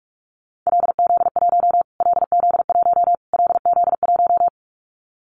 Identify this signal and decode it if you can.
Morse code, heard as a keyed tone, about 44 wpm, 700 Hz. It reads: LZ1 LZ1 LZ1
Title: Callsign Prefixes for Top 25 DXCC - Repeated 3x - 45wpm